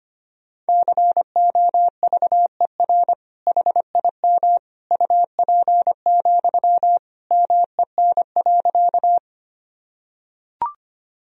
Morse code, heard as a keyed tone, about 25 wpm, 700 Hz. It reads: COVER HIM UP, MEN.
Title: Sentences from Top 1000 Words - Review Part 1 - 25wpm